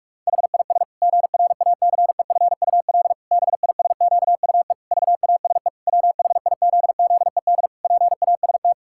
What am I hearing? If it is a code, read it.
Morse code, heard as a keyed tone, about 41 wpm, 700 Hz. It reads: HIS GRACEFUL BISQUE VASE WHIZZED PAST
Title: Pangram Challenge v2 - 45wpm